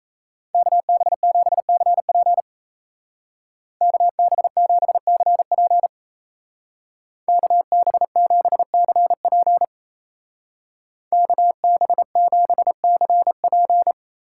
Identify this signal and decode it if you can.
Morse code, heard as a keyed tone, about 35 wpm, 700 Hz. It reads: KB7CP KB7CP KB7CP KB7CP
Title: Speed Racing - USA 2x2 Call Signs 28wpm